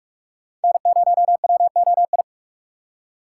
Morse code, heard as a keyed tone, about 46 wpm, 700 Hz. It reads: N0WOI
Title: K1USN SST QSO Practice - 45wpm